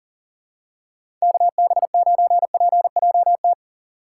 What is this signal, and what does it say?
Morse code, heard as a keyed tone, about 40 wpm, 700 Hz. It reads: KB9PJT
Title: USA Call Signs - 2x Character Spacing - 40wpm